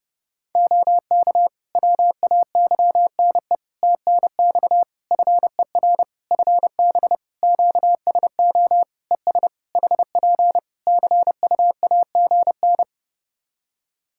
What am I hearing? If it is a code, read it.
Morse code, heard as a keyed tone, about 30 wpm, 700 Hz. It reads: OK WAYNE TNX FER FB QSO ES HP CUAGN
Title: Practice QSO - Exchange 4 - Ending - 30wpm